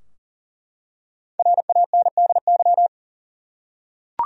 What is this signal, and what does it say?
Morse code, heard as a keyed tone, about 37 wpm, 700 Hz. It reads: RANDY
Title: USA Common Names - Max 5 Letters - 40wpm